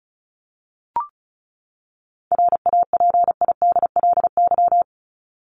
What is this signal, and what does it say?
Morse code, heard as a keyed tone, about 34 wpm, 700 Hz. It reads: RAPIDLY
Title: LY Suffix - Top 100 Words with Base Words 35wpm